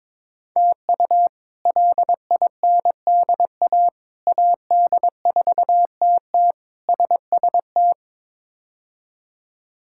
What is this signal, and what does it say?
Morse code, heard as a keyed tone, about 21 wpm, 700 Hz. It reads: TU LINDA AD4TT SST
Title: K1USN SST QSO Practice - 22wpm